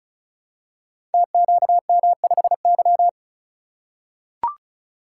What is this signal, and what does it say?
Morse code, heard as a keyed tone, about 36 wpm, 700 Hz. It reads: TQM5Y
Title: Code Groups - 5 Letter:Numbers - 35wpm